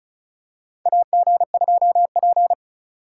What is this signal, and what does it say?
Morse code, heard as a keyed tone, about 36 wpm, 700 Hz. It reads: AG2P
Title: USA Call Signs - 4 Characters - 3x Character Spacing - 35wpm